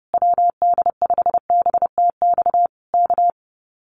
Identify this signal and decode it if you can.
Morse code, heard as a keyed tone, about 30 wpm, 700 Hz. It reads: WD5BTX K
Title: Common QSO Phrases Encore 30wpm